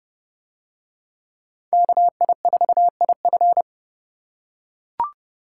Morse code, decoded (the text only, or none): KI4IF